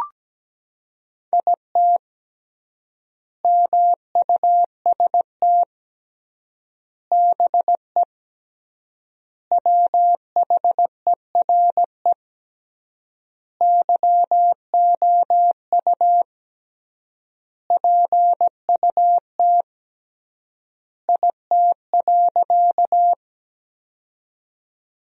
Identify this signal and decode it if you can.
Morse code, heard as a keyed tone, about 17 wpm, 700 Hz. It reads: IT MUST BE WHERE YOU PUT IT.
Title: 3x Word-Spacing - Sentences from Top 200 Words - 17wpm